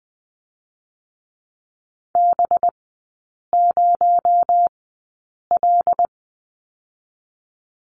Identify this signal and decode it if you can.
Morse code, heard as a keyed tone, about 19 wpm, spaced as for 8 wpm, 700 Hz. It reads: B0L
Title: 218 Lesson 35e - Number 0 - Three Characters 20wpm